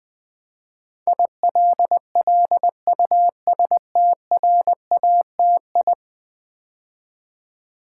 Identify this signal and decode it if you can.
Morse code, heard as a keyed tone, about 20 wpm, 700 Hz. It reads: ILLUSTRATI
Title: Build A Word - 5 Letters At A Time - 12-Letter Words - 20wpm